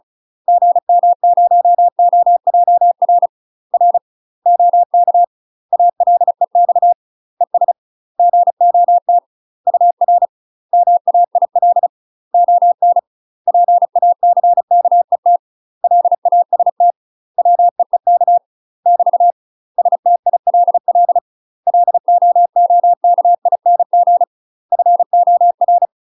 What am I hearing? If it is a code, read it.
Morse code, heard as a keyed tone, about 35 wpm, 700 Hz. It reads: GM0OJR R OK ALEX ES GOT UR MAIL ON PACKET LAST WEEK <BT> STILL LOOKING FOR
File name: UK MASTER QSOs TRAINING/35WPM